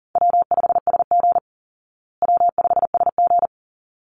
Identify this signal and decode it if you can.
Morse code, heard as a keyed tone, about 40 wpm, 700 Hz. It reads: W5SG W5SG
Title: Easy Call Signs Repeated 2x - 40wpm